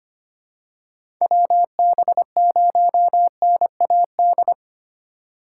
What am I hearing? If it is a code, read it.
Morse code, heard as a keyed tone, about 25 wpm, 700 Hz. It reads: WB0NAD
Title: USA Call Signs - 4x Character Spacing - 25wpm